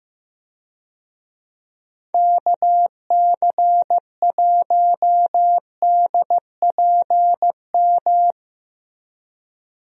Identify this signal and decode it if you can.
Morse code, heard as a keyed tone, about 15 wpm, 700 Hz. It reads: KC1DPM
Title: USA Call Signs - 6 Characters - Repeated 3x - 15wpm